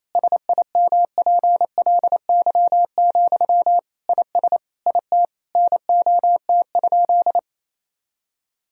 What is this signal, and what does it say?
Morse code, heard as a keyed tone, about 28 wpm, 700 Hz. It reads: SIMPLY, IS IT NOT?
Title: Sentences from Top 1000 Words - Review Part 4 - 28wpm